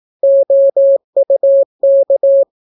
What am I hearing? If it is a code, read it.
Morse code, heard as a keyed tone, about 18 wpm, 550 Hz. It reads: OUK